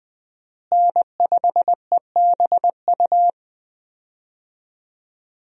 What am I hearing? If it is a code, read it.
Morse code, heard as a keyed tone, about 21 wpm, 700 Hz. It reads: N5EBU
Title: USA Call Signs - 5 Characters - Repeated 3x - 20wpm